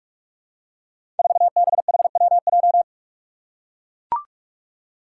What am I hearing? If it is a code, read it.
Morse code, heard as a keyed tone, about 42 wpm, 700 Hz. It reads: 4BHWJ
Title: Five Challenging Letter-Number Combinations 45wpm